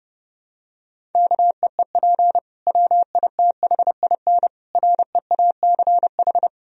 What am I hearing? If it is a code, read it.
Morse code, heard as a keyed tone, about 32 wpm, 700 Hz. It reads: KEEP WITHIN REACH
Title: Sentences from Top 1000 Words - Review Part 4 - 30wpm